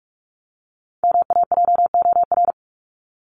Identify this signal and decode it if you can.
Morse code, heard as a keyed tone, about 44 wpm, 700 Hz. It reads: MAJOR
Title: Top 501-750 Words 45wpm